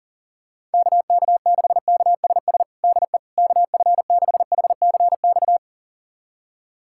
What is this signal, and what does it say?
Morse code, heard as a keyed tone, about 37 wpm, 700 Hz. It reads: KK6KSS DE KF6HCX
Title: USA Callsign DE Callsign - 6 Characters - 40wpm